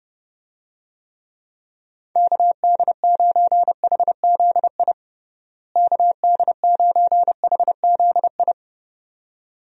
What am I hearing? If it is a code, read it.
Morse code, heard as a keyed tone, about 32 wpm, 700 Hz. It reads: KD9HZI KD9HZI
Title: USA Call Signs - 6 Characters - Repeated 2x - 30wpm